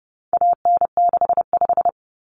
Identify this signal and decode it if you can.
Morse code, heard as a keyed tone, about 30 wpm, 700 Hz. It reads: AN65
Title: Common Cut Numbers - 4-Digit 30wpm